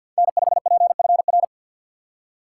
Morse code, heard as a keyed tone, about 52 wpm, 690 Hz. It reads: N5PFR
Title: USA Call Signs - 2x Character Spacing - 50wpm